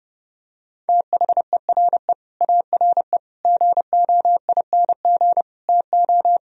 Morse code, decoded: THERE ARE GOING TO

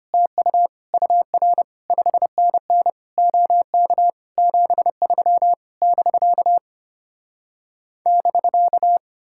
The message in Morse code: TU UR 5NN OK 73 <BK> <BK>

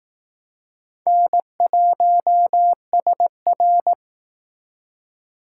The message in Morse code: N1SR

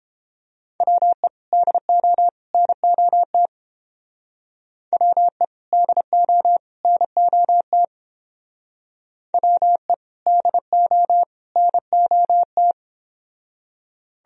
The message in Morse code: WE DO NOT WE DO NOT WE DO NOT

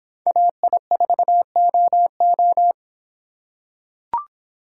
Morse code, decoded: AI4OO

